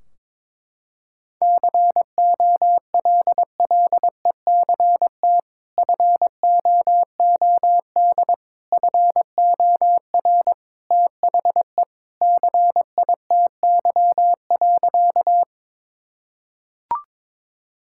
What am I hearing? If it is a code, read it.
Morse code, heard as a keyed tone, about 22 wpm, 700 Hz. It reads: COLLECT FOOD FOR THE CITY.
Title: Sentences from Top 1600 Words 22wpm